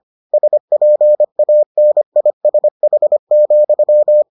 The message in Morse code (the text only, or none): SPANISH,